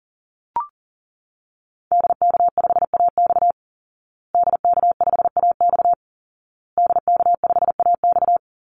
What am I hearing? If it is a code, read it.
Morse code, heard as a keyed tone, about 40 wpm, 700 Hz. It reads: DK5AX DK5AX DK5AX
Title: Call Signs Repeated 2x - 40wpm